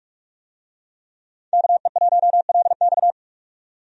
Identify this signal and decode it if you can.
Morse code, heard as a keyed tone, about 41 wpm, 700 Hz. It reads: KE1LX
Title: USA Call Signs - 5 Characters - 45wpm